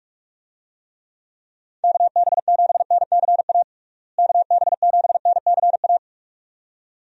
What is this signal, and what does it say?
Morse code, heard as a keyed tone, about 46 wpm, 700 Hz. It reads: KB7NCA KB7NCA
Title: USA Call Signs - 6 Characters - Repeated 2x - 45wpm